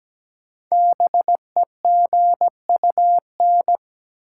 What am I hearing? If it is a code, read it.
Morse code, heard as a keyed tone, about 17 wpm, 700 Hz. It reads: BEGUN
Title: Sentences from Top 1000 Words - Review Part 6 - New Word Drill - 17wpm